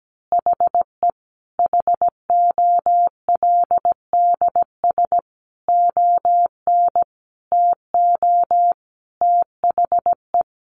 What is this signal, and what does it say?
Morse code, heard as a keyed tone, about 17 wpm, 700 Hz. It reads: HE HOLDS ON TO THE
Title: Additional Easy Phrases 17wpm